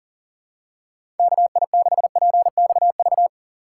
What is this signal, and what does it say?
Morse code, heard as a keyed tone, about 40 wpm, 700 Hz. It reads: KI6PXV
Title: USA Call Signs - 6 Characters - 3x Character Spacing - 40wpm